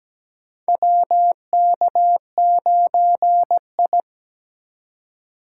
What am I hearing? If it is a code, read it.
Morse code, heard as a keyed tone, about 17 wpm, 700 Hz. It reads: WK9I
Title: USA Call Signs - 4x Character Spacing - 17wpm